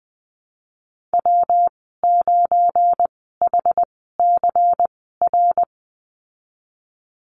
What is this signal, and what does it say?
Morse code, heard as a keyed tone, about 20 wpm, spaced as for 15 wpm, 700 Hz. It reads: W9HCR
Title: 213 Lesson 34g - Number 8 - Call Signs 20wpm